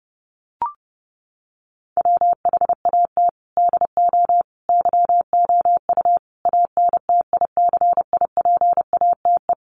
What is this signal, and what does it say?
Morse code, heard as a keyed tone, about 30 wpm, 700 Hz. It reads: WHAT DO YOU ANTICIPATE
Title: Sentences from Top 1500 Words 30wpm